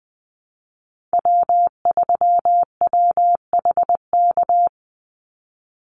Morse code, heard as a keyed tone, about 20 wpm, 700 Hz. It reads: W3WHK
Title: Call Signs - 20wpm